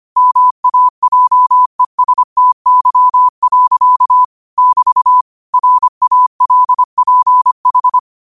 Morse code, heard as a keyed tone, about 25 wpm, 1000 Hz. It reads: MAJESTY. <BT> RALPH
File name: qotdfiles / 25wpm